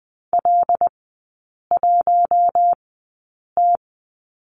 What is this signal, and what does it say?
Morse code, heard as a keyed tone, about 20 wpm, spaced as for 8 wpm, 700 Hz. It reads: L1T
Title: 100 Lesson 17e - Letter u - Three Characters 20wpm